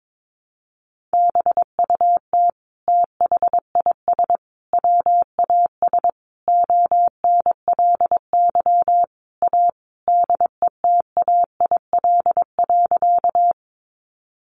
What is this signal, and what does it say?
Morse code, heard as a keyed tone, about 22 wpm, 700 Hz. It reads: BUT THIS WAS ONLY A DETAIL.
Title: Sentences from Top 1500 Words 22wpm